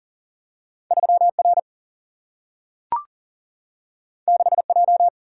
Morse code, heard as a keyed tone, about 39 wpm, 700 Hz. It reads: 3R 6J
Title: Single-Letter Single-Number 40wpm